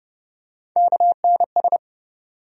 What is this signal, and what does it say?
Morse code, heard as a keyed tone, about 30 wpm, 700 Hz. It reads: KNS
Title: USA Call Sign Postfixes - 3 Letters 30wpm